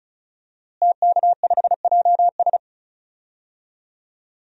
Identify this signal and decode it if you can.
Morse code, heard as a keyed tone, about 36 wpm, 700 Hz. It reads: TK5JS
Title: DXCC-100 Call Signs - 5 Characters - Repeated 3x - 35wpm